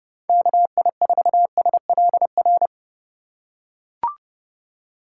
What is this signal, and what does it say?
Morse code, heard as a keyed tone, about 29 wpm, 700 Hz. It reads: KI4SLR
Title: USA Call Signs - 6 Characters - 2x Character Spacing - 30wpm